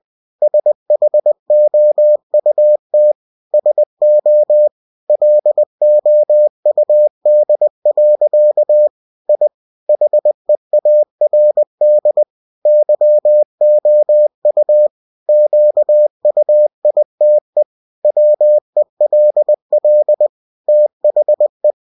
Morse code, heard as a keyed tone, about 20 wpm, 600 Hz. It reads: SHOUT SO LOUD. I HEARD YOU QUITE WELL THE
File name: cw-books / winnie-the-pooh / winnie-20wpm-page-0013